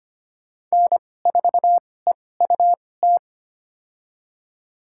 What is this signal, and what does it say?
Morse code, heard as a keyed tone, about 25 wpm, spaced as for 18 wpm, 700 Hz. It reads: N4EUT
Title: USA Call Signs - 5 Characters - 2x Character Spacing - 25wpm